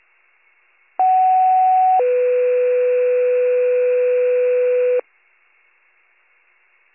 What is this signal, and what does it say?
Two-tone page, heard as a one-second tone, then a three-second tone, about 750 then 500 Hz. It is heard from 1.0 to 5.0 s.